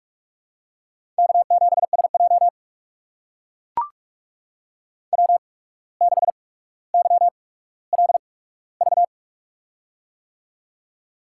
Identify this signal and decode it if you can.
Morse code, heard as a keyed tone, about 44 wpm, spaced as for 18 wpm, 700 Hz. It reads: K7SJ W6YLV
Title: USA Call Signs - 8x Character Spacing - 45wpm